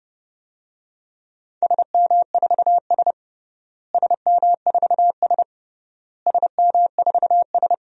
Morse code, heard as a keyed tone, about 30 wpm, 700 Hz. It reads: SM4S SM4S SM4S
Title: DXCC-100 Call Signs - 4 Characters - Repeated 3x - 30wpm